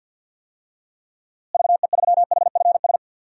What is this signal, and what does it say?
Morse code, heard as a keyed tone, about 48 wpm, 700 Hz. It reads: VE3HFS
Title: DXCC-100 Call Signs - 6 Characters - Repeated 3x - 50wpm